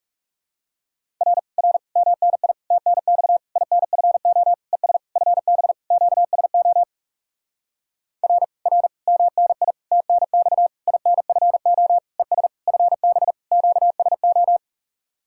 Morse code, heard as a keyed tone, about 40 wpm, 700 Hz. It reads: R R MNI TNX INFO ES FB QSO R R MNI TNX INFO ES FB QSO